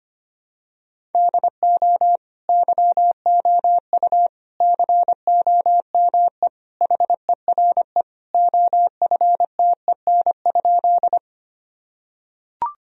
Morse code, heard as a keyed tone, about 25 wpm, 700 Hz. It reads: DO YOU COME HERE OFTEN?
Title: Sentences from Top 1000 Words - Review Part 1 - 25wpm